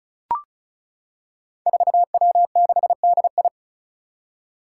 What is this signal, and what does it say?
Morse code, heard as a keyed tone, about 34 wpm, 700 Hz. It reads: 4W6DI
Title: Five Challenging Letter-Number Combinations 35wpm